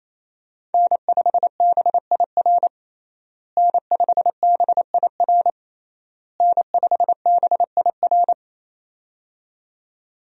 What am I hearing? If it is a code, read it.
Morse code, heard as a keyed tone, about 27 wpm, 700 Hz. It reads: N5BIR N5BIR N5BIR
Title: USA Call Signs - 5 Characters - Repeated 3x - 28wpm